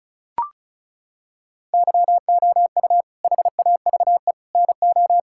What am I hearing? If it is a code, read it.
Morse code, heard as a keyed tone, about 34 wpm, 700 Hz. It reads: YOU HAVE NO
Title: Sentences from Top 1000 Words - Review Part 5 - 35wpm